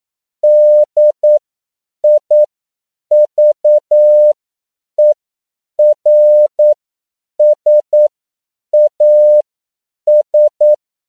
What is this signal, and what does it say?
Morse code, heard as a keyed tone, about 9 wpm, spaced as for 7 wpm, 600 Hz. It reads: DIVERSAS
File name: CW / 10PPM